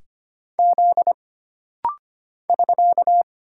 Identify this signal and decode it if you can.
Morse code, heard as a keyed tone, about 25 wpm, 700 Hz. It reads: Z <SK>
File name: Single Character (Letter:Number:Punctuation:Prosign) - Rapid-Fire - 25wpm